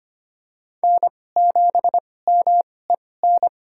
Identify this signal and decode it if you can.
Morse code, heard as a keyed tone, about 25 wpm, spaced as for 18 wpm, 700 Hz. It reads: N7MEN
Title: USA Call Signs - 5 Characters - 2x Character Spacing - 25wpm